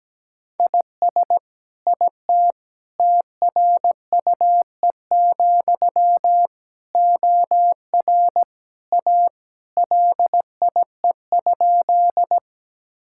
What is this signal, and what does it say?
Morse code, heard as a keyed tone, about 17 wpm, 700 Hz. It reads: IS IT TRUE, OR A LIE?
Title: Sentences from Top 1600 Words 17wpm